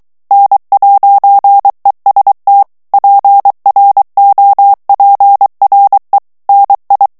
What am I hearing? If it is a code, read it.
Morse code, heard as a keyed tone, about 23 wpm, 800 Hz. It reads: N'EST PROPRE NI
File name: morse mp3 / FRANCAIS TEXTE CLAIR 2 VITESSE 22 MOTS MINUTES